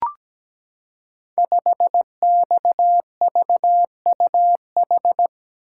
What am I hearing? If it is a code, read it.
Morse code, heard as a keyed tone, about 17 wpm, 700 Hz. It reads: 5XVUH
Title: Code Groups - 5 Letter:Numbers - 17wpm